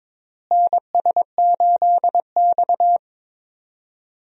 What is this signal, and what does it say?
Morse code, heard as a keyed tone, about 22 wpm, 700 Hz. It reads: NS8X
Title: USA Call Signs - 4 Characters - Repeated 2x - 22wpm